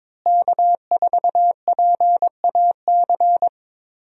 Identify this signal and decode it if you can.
Morse code, heard as a keyed tone, about 22 wpm, 700 Hz. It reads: K4PAC
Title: USA Call Signs - 5 Characters - Repeated 3x - 22wpm